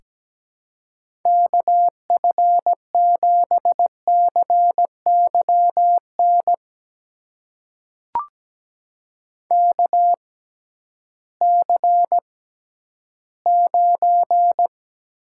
Morse code, heard as a keyed tone, about 17 wpm, spaced as for 11 wpm, 700 Hz. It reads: KF7CYN KC9
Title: USA Call Signs - 6 Characters - 6x Character Spacing - 17wpm